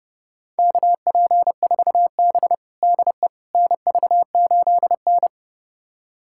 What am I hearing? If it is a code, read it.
Morse code, heard as a keyed tone, about 30 wpm, 700 Hz. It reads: KP4B DE NV8N
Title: USA Callsign DE Callsign - 4 Characters - 30wpm